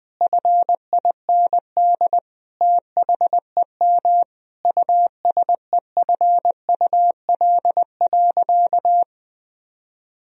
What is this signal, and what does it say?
Morse code, heard as a keyed tone, about 20 wpm, 700 Hz. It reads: FIND THEM USEFUL.
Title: Sentences from Top 1000 Words - Review Part 4 - 20wpm